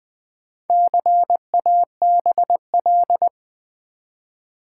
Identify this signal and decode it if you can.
Morse code, heard as a keyed tone, about 20 wpm, 700 Hz. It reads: CABL
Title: Build A Word - 1 Letter At A Time - 3 to 5-Letter Words - 20wpm